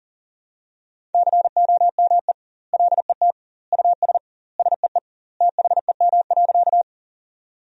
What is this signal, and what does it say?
Morse code, heard as a keyed tone, about 40 wpm, 700 Hz. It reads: COME LET US SEE THEM.
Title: 2x Word-Spacing - Sentences from Top 300 Words - 40wpm